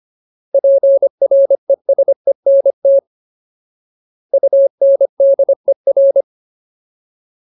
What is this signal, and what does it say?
Morse code, heard as a keyed tone, about 25 wpm, 550 Hz. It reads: PRESENT UNDER
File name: Ultimate Word List / 25WPM Ultimate